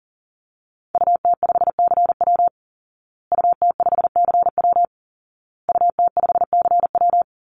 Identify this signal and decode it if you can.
Morse code, heard as a keyed tone, about 40 wpm, 700 Hz. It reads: UT5CW UT5CW UT5CW
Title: Call Signs Repeated 2x - 40wpm